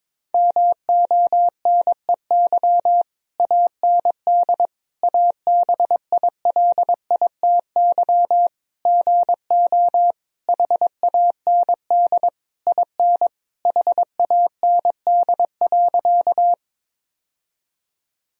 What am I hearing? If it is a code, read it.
Morse code, heard as a keyed tone, about 22 wpm, 700 Hz. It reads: MONEY AND ABILITY GO HAND IN HAND.
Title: Sentences from Top 1000 Words - Review Part 3 - 22wpm